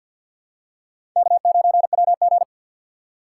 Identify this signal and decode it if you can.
Morse code, heard as a keyed tone, about 52 wpm, 700 Hz. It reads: K9WG
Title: USA Call Signs - 4 Characters - 3x Character Spacing - 50wpm